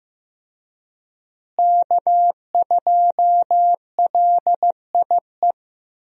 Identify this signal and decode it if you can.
Morse code, heard as a keyed tone, about 15 wpm, 700 Hz. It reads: K2LIE
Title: USA Call Signs - 5 Characters - Repeated 2x - 15wpm